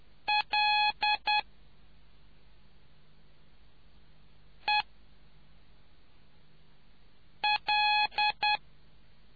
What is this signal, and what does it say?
Morse code, heard as a keyed tone, about 10 wpm, 840 Hz. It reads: L E L